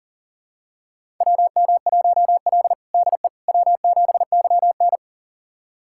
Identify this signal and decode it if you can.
Morse code, heard as a keyed tone, about 37 wpm, 700 Hz. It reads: WM1L DE W7YN
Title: USA Callsign DE Callsign - 4 Characters - 40wpm